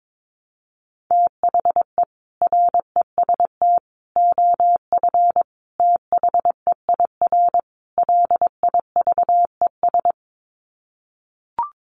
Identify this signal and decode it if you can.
Morse code, heard as a keyed tone, about 22 wpm, 700 Hz. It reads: THE REST OF THEIR LIVES
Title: Sets of 5 Words Encore 22wpm